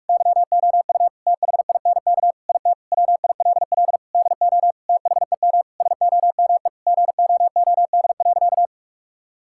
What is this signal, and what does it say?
Morse code, heard as a keyed tone, about 47 wpm, 700 Hz. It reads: YOU THINK IT WILL DO THEM SOME GOOD.
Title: Sentences from Top 1000 Words - Review Part 6 - 45wpm